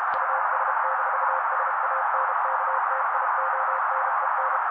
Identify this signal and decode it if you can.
Morse code, heard as a keyed tone, about 32 wpm, 530 Hz. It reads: US-SANKTIONEN
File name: daily / 31wpm noise